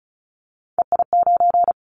Morse code, heard as a keyed tone, about 36 wpm, 690 Hz. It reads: EI9
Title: Callsign Prefixes for Top 50 DXCC - 35wpm